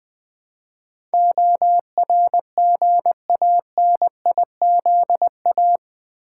Morse code, heard as a keyed tone, about 20 wpm, 700 Hz. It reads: ORGANIZA
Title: Build A Word - 4 Letters At A Time - 12-Letter Words - 20wpm